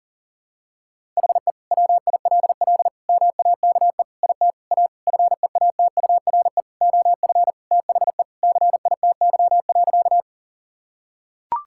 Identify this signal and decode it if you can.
Morse code, heard as a keyed tone, about 37 wpm, 700 Hz. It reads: HE WILL MAKE IT A FEATURE OF THE CITY.
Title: Sentences from Top 1000 Words - Review Part 5 - 40wpm